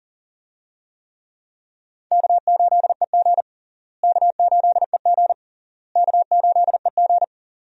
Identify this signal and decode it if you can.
Morse code, heard as a keyed tone, about 40 wpm, 700 Hz. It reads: K8EG K8EG K8EG
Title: USA Call Signs - 4 Characters - Repeated 3x - 40wpm